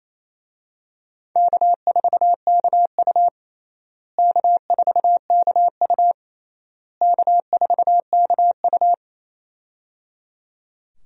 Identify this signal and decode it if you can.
Morse code, heard as a keyed tone, about 28 wpm, 700 Hz. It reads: K4KU K4KU K4KU
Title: USA Call Signs - 4 Characters - Repeated 3x - 28wpm